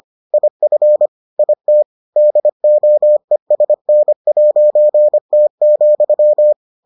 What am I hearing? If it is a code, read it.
Morse code, heard as a keyed tone, about 25 wpm, 600 Hz. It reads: IF IT DOESN'T,